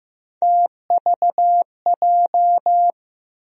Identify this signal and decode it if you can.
Morse code, heard as a keyed tone, about 15 wpm, 700 Hz. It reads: TVJ
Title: USA Call Sign Postfixes - 3 Letters 15wpm